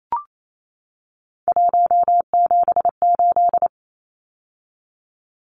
Morse code, heard as a keyed tone, about 28 wpm, 700 Hz. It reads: 178
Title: Numbers - 3-Digit 28wpm